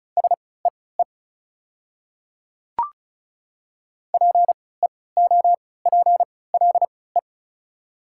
Morse code, heard as a keyed tone, about 33 wpm, spaced as for 20 wpm, 690 Hz. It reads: SEE PEOPLE
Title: 3x Character Spacing - Top 100 Words - 35wpm